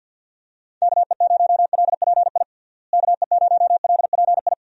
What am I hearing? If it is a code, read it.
Morse code, heard as a keyed tone, about 52 wpm, 700 Hz. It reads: KE0LPI KE0LPI
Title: USA Call Signs - 6 Characters - Repeated 3x - 50wpm